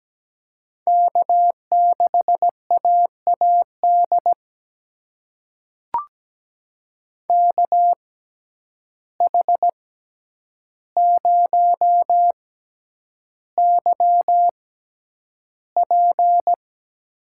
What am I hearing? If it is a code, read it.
Morse code, heard as a keyed tone, about 17 wpm, spaced as for 9 wpm, 700 Hz. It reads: K6AAD KH0YP
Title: USA Call Signs - 5 Characters - 6x Character Spacing - 17wpm